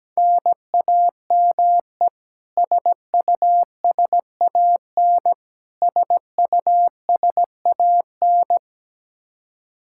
Morse code, heard as a keyed tone, about 17 wpm, 700 Hz. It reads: NAME SUSAN SUSAN
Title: Practice QSO - Exchange 1 - Beginning - 17wpm